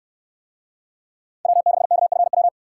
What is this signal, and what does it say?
Morse code, heard as a keyed tone, about 69 wpm, 700 Hz. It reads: W6KFW